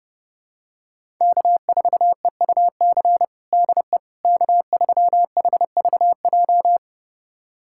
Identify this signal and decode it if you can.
Morse code, heard as a keyed tone, about 32 wpm, 700 Hz. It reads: K4EUC DE K3HVJ
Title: USA Callsign DE Callsign - 5 Characters - 30wpm